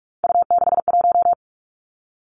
Morse code, heard as a keyed tone, about 46 wpm, 700 Hz. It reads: U61
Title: Three Challenging Letter-Number Combinations 45wpm